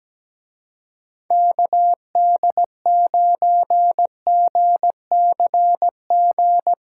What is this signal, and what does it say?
Morse code, heard as a keyed tone, about 17 wpm, 700 Hz. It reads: KD9GCG
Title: Speed Racing - USA 2x3 Call Signs 17wpm